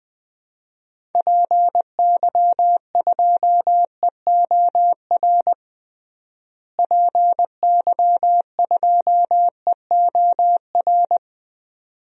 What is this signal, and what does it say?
Morse code, heard as a keyed tone, about 19 wpm, 700 Hz. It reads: PY2EOR PY2EOR
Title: DXCC-100 Call Signs - 6 Characters - Repeated 2x - 20wpm